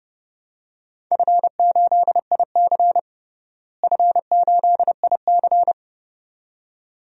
Morse code, heard as a keyed tone, about 30 wpm, 700 Hz. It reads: F8IC F8IC
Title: DXCC-100 Call Signs - 4 Characters - Repeated 2x - 30wpm